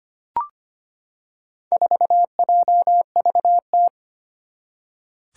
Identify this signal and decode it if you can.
Morse code, heard as a keyed tone, about 25 wpm, 700 Hz. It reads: 4JVT
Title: Four Challenging Letter-Number Combinations 25wpm